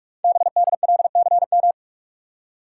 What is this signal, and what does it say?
Morse code, heard as a keyed tone, about 44 wpm, 690 Hz. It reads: BDLCM
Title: Code Groups - 5 Letters - 45wpm